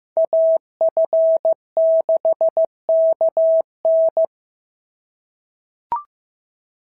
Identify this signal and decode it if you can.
Morse code, heard as a keyed tone, about 14 wpm, 650 Hz. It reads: AF6KN
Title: SOTA QSO Practice 15wpm